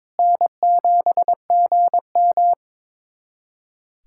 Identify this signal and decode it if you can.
Morse code, heard as a keyed tone, about 21 wpm, 700 Hz. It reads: N7GM
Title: USA Callsign DE Callsign - 4 Characters - 22wpm